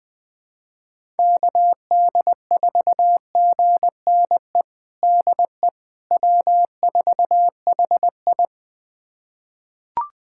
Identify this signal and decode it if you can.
Morse code, heard as a keyed tone, about 20 wpm, 700 Hz. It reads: KD4GNE DE W4HI
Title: USA Callsign DE Callsign 20wpm